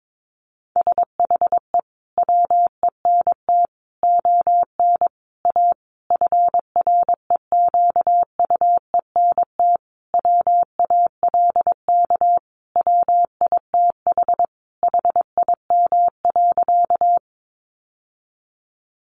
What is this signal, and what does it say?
Morse code, heard as a keyed tone, about 22 wpm, 700 Hz. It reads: SHE WENT ON A FREQUENT WALK WITH HIM.
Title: Sentences from Top 1400 Words 22wpm